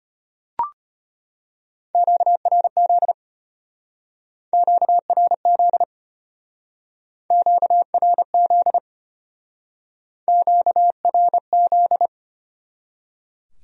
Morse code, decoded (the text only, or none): QRZ QRZ QRZ QRZ